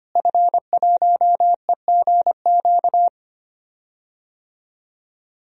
Morse code, heard as a keyed tone, about 25 wpm, 700 Hz. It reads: F1EGQ
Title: DXCC-100 Call Signs - 5 Characters - Repeated 2x - 25wpm